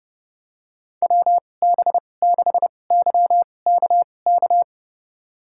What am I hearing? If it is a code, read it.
Morse code, heard as a keyed tone, about 30 wpm, spaced as for 22 wpm, 700 Hz. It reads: WB6YKK
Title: USA Call Signs - 2x Character Spacing - 30wpm